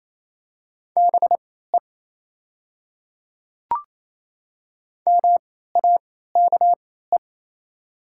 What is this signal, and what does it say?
Morse code, heard as a keyed tone, about 28 wpm, spaced as for 16 wpm, 700 Hz. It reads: BE MAKE
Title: 3x Character Spacing - Top 300 Words - 28wpm